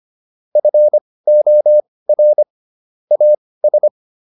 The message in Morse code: FOR AS

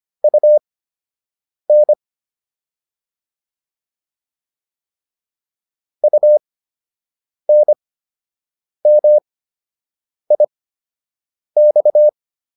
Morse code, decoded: UN UNMIX

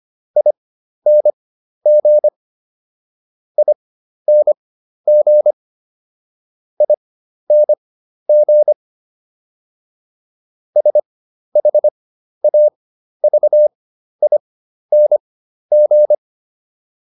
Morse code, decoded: ING ING ING SHAVING

